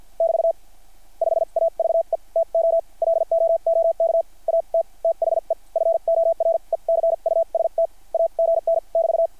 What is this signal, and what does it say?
Morse code, heard as a keyed tone, about 50 wpm, spaced as for 42 wpm, 660 Hz. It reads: <BT> 5AVE TO LOOK AT THE FORECUST AGN <BT>
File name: archived / CRSnet